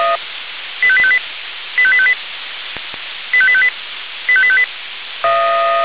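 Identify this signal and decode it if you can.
Five-tone selective call (ZVEI), heard as quick runs of five tones, tones of about 70 ms. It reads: 85858 85858 85858 85858
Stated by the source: ZVEI five-tone call received by SDR.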